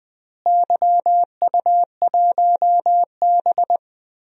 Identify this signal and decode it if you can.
Morse code, heard as a keyed tone, about 20 wpm, 700 Hz. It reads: YU1B
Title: DXCC-100 Call Signs - 4 Characters - Repeated 3x - 20wpm